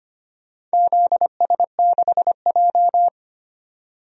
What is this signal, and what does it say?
Morse code, heard as a keyed tone, about 25 wpm, 700 Hz. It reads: ZS6J
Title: DXCC-100 Call Signs - 4 Characters - 25wpm